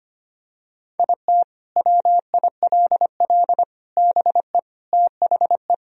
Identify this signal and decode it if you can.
Morse code, heard as a keyed tone, about 25 wpm, 700 Hz. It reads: IT WILL BE THE
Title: Sentences from Top 1000 Words - Review Part 2 - 25wpm